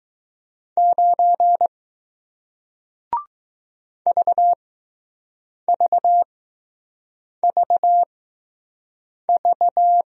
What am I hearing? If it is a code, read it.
Morse code, heard as a keyed tone, about 20 wpm, spaced as for 6 wpm, 700 Hz. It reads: 9 VVVV